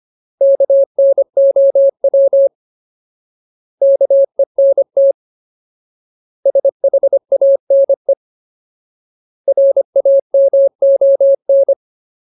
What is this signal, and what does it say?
Morse code, heard as a keyed tone, about 25 wpm, 550 Hz. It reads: KNOW KENT SHANE RAMON
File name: Ultimate Word List / 25WPM Ultimate